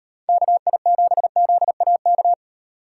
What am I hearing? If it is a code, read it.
Morse code, heard as a keyed tone, about 38 wpm, 700 Hz. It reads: KI7ZAK